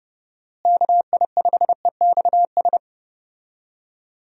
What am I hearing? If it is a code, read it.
Morse code, heard as a keyed tone, about 30 wpm, 700 Hz. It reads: KI5EXS
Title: USA Call Signs - 6 Characters - Repeated 3x - 30wpm